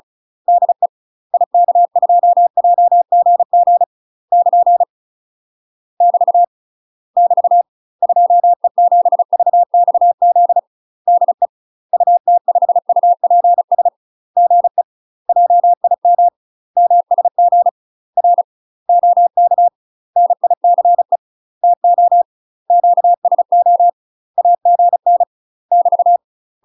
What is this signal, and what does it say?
Morse code, heard as a keyed tone, about 34 wpm, 700 Hz. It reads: DE IK2JGG <KN> <BT> <BT> 2E7VXZ DE UT5UPS GE JIM MSG R OK NICE TO QSO AGN <BT>
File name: UK MASTER QSOs TRAINING / 35WPM / UK MASTER QSOs 0001